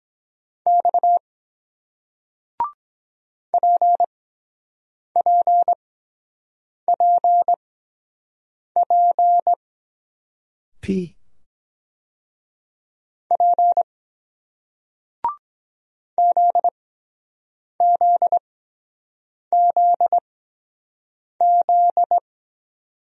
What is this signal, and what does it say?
Morse code, heard as a keyed tone, about 23 wpm, spaced as for 7 wpm, 700 Hz. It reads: X PPPP P ZZZZ